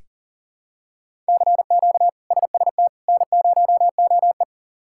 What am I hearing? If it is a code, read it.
Morse code, heard as a keyed tone, about 40 wpm, 700 Hz. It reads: CQ SST N0OE
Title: K1USN SST QSO Practice - 40wpm